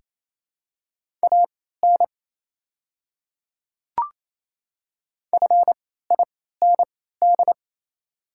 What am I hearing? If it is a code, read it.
Morse code, heard as a keyed tone, about 28 wpm, spaced as for 16 wpm, 700 Hz. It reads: AN FIND
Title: 3x Character Spacing - Top 300 Words - 28wpm